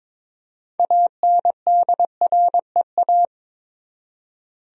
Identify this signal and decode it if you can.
Morse code, heard as a keyed tone, about 22 wpm, 700 Hz. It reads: ANDREA
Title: USA Common Names - 22wpm